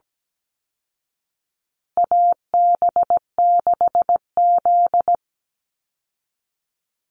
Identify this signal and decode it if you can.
Morse code, heard as a keyed tone, about 17 wpm, 700 Hz. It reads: AB6Z
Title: Call Signs Repeated 2x - 17wpm